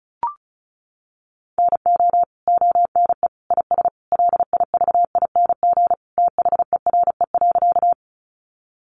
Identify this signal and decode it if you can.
Morse code, heard as a keyed tone, about 34 wpm, 700 Hz. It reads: NO ONE IS LIVING THERE.
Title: Sentences from Top 1100 Words 35wpm